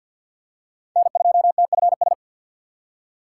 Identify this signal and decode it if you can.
Morse code, heard as a keyed tone, about 48 wpm, 700 Hz. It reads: N2TFS
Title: USA Call Signs - 5 Characters - 6x Character Spacing - 50wpm